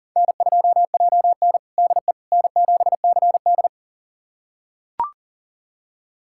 Morse code, heard as a keyed tone, about 40 wpm, 700 Hz. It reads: N2JN DE N7CD
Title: USA Callsign DE Callsign - 4 Characters - 40wpm